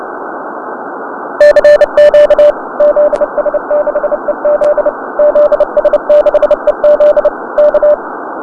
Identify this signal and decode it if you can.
Morse code, heard as a keyed tone, about 30 wpm, 590 Hz. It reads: CQ ZS6EZ ZS6EZ K
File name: 2014-3-19-zs6ez-cw-cq